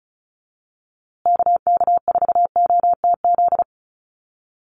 Morse code, heard as a keyed tone, about 34 wpm, 700 Hz. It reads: KK4OTZ
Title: USA Call Signs Repeated 3x 35wpm